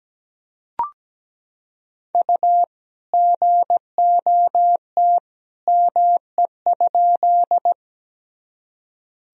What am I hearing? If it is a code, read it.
Morse code, heard as a keyed tone, about 17 wpm, 700 Hz. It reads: U GOT ME?
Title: Sentences from Top 300 Words and Abbreviations 17wpm